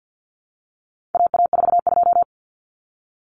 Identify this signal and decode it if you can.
Morse code, heard as a keyed tone, about 50 wpm, 700 Hz. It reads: AA42
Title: Common Cut Numbers - 4-Digit 50wpm